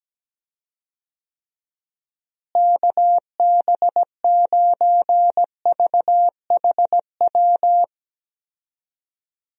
Morse code, heard as a keyed tone, about 17 wpm, 700 Hz. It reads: KB9VHW
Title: USA Call Signs - 6 Characters - Repeated 2x - 17wpm